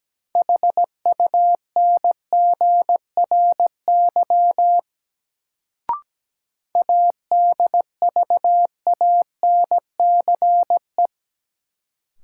Morse code, heard as a keyed tone, about 17 wpm, 700 Hz. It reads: HUNGRY ADVANCE
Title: Sentences from Top 1400 Words - New Word Drill - 17wpm